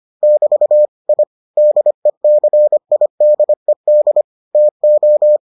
Morse code, heard as a keyed tone, about 25 wpm, 600 Hz. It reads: <BT> I DECIDED TO